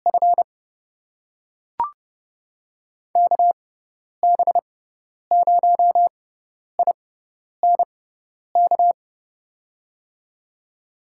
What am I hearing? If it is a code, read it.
Morse code, heard as a keyed tone, about 30 wpm, spaced as for 10 wpm, 700 Hz. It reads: F KB0INK
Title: USA Call Signs - 6 Characters - 6x Character Spacing - 30wpm